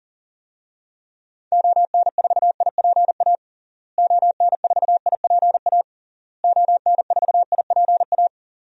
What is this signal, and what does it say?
Morse code, heard as a keyed tone, about 40 wpm, 700 Hz. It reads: ON4IPA ON4IPA ON4IPA
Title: DXCC-100 Call Signs - 6 Characters - Repeated 3x - 40wpm